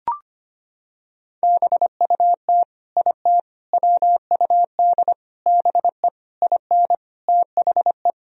Morse code, heard as a keyed tone, about 25 wpm, 700 Hz. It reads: BUT IT WUD BE IN THE
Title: Sentences from Top 300 Words and Abbreviations 25wpm